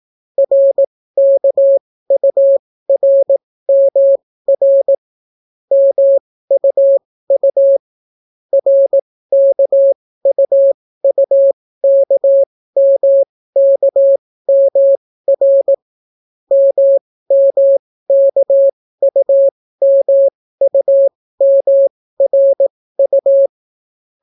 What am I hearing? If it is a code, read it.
Morse code, heard as a keyed tone, about 18 wpm, spaced as for 14 wpm, 550 Hz. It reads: RKURMR MUU RKUUKMKMR MMKUMUMRU